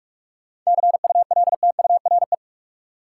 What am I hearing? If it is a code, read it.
Morse code, heard as a keyed tone, about 44 wpm, 700 Hz. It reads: CULTURE
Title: Seven Letter Words - 45wpm